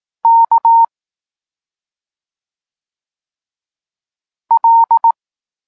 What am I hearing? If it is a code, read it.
Morse code, heard as a keyed tone, about 18 wpm, 920 Hz. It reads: K L
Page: Tekst 32 – Doskonalenie odbioru poznanych liter